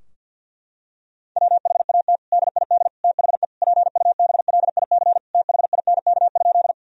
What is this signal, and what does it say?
Morse code, heard as a keyed tone, about 50 wpm, 700 Hz. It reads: WHAT DID THE PUBLIC THINK?
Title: Sentences from Top 1000 Words - Review Part 3 - 50wpm